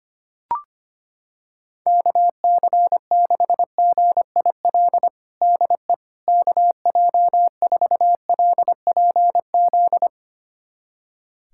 Morse code, heard as a keyed tone, about 25 wpm, 700 Hz. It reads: KC6GIL DE KJ4LPZ
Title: USA Callsign DE Callsign - 6 Characters - 25wpm